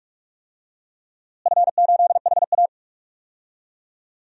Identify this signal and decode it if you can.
Morse code, heard as a keyed tone, about 46 wpm, 700 Hz. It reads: F8HA
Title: DXCC-100 Call Signs - 4 Characters - Repeated 2x - 45wpm